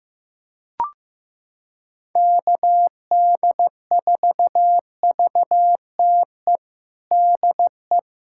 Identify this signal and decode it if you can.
Morse code, heard as a keyed tone, about 15 wpm, 700 Hz. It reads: KD4VTE DE
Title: USA Callsign DE Callsign - 6 Characters - 15wpm